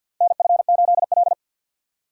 Morse code, heard as a keyed tone, about 51 wpm, 700 Hz. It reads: NF8L
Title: USA Call Signs - 3x Character Spacing - 50wpm